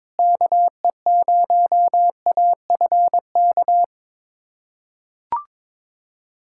Morse code, heard as a keyed tone, about 22 wpm, 700 Hz. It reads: KE0AFK
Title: USA Callsign DE Callsign 22wpm